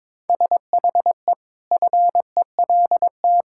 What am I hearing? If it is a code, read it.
Morse code, heard as a keyed tone, about 22 wpm, 700 Hz. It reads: SHE FELT
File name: practice / 22wpm / Sentences with 8-Word Phrases - 22wpm